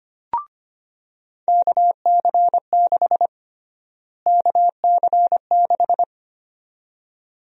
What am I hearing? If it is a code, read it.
Morse code, heard as a keyed tone, about 25 wpm, 700 Hz. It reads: KC6 KC6
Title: USA Call Sign Prefixes - Repeated 2x - 25wpm